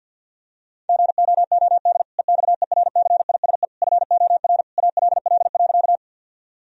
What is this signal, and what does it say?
Morse code, heard as a keyed tone, about 50 wpm, 700 Hz. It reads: GOOD EXERCISE FOR ALL.
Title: Sentences from Top 1000 Words - Review Part 4 - 50wpm